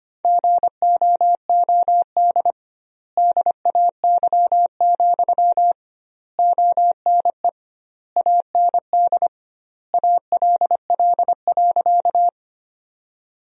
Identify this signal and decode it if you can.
Morse code, heard as a keyed tone, about 25 wpm, 700 Hz. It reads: GOOD DAY, ONE AND ALL.
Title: 2x Word-Spacing - Sentences from Top 200 Words - 25wpm